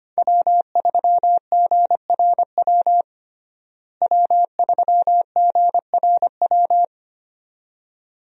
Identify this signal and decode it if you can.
Morse code, heard as a keyed tone, about 25 wpm, 700 Hz. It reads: W3GRW W3GRW
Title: USA Call Signs - 5 Characters - Repeated 2x - 25wpm